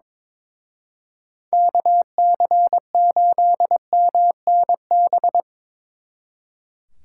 Morse code, heard as a keyed tone, about 25 wpm, 700 Hz. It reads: KC8MNB
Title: Speed Racing - USA 2x3 Call Signs 22wpm